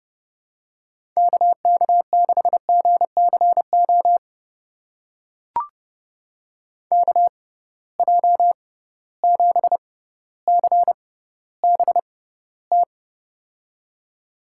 Morse code, heard as a keyed tone, about 32 wpm, spaced as for 16 wpm, 700 Hz. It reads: KK6GCO KJ7CBT
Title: USA Call Signs - 6x Character Spacing - 30wpm